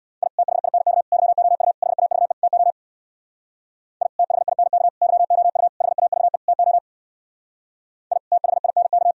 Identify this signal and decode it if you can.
Morse code, heard as a keyed tone, about 62 wpm, 700 Hz. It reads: I THINK YOU HAVE TO I THINK YOU HAVE TO I THINK
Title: Speed-Racing - Sets of 5 Words 50wpm